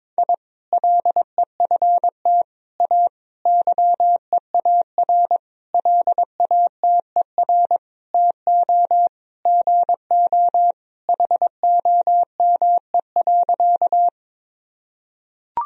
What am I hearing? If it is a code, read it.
Morse code, heard as a keyed tone, about 22 wpm, 700 Hz. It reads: I LEFT A YEAR LATER TO GO HOME.
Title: Sentences from Top 1000 Words - Review Part 4 - 22wpm